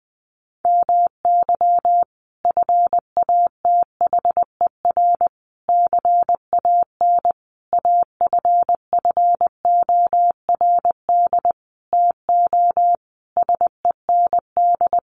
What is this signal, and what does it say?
Morse code, heard as a keyed tone, about 20 wpm, 700 Hz. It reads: MY FATHER CAN AFFORD TO SEND
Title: Sentences from Top 1500 Words 20wpm